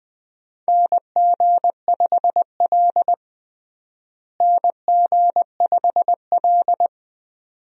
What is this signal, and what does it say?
Morse code, heard as a keyed tone, about 20 wpm, 700 Hz. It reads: NG5L NG5L
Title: USA Call Signs - 4 Characters - Repeated 2x - 20wpm